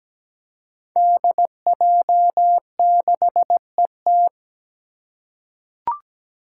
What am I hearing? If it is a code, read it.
Morse code, heard as a keyed tone, about 17 wpm, 700 Hz. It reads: DJ6ET
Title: Five Challenging Letter-Number Combinations 17wpm